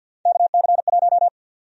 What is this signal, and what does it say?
Morse code, heard as a keyed tone, about 53 wpm, 700 Hz. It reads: KC1
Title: USA Call Sign Prefixes - 50wpm